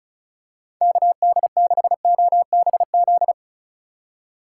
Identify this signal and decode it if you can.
Morse code, heard as a keyed tone, about 36 wpm, 700 Hz. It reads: KD6OBZ
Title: USA Call Signs - 6 Characters - 4x Character Spacing - 35wpm